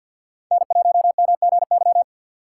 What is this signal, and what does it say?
Morse code, heard as a keyed tone, about 52 wpm, 700 Hz. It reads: N1MGY
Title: USA Call Signs - 5 Characters - 3x Character Spacing - 50wpm